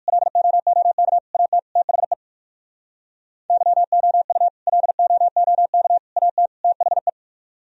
Morse code, heard as a keyed tone, about 57 wpm, 700 Hz. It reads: LOOK AT THE YOU LOOK AT THE